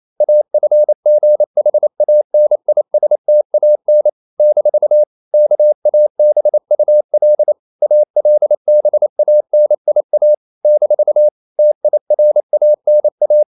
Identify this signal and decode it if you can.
Morse code, heard as a keyed tone, about 28 wpm, 600 Hz. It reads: AFGHANISTAN - KABUL ALBANIA - TIRANA
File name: CW / Practice / NationalCapitals28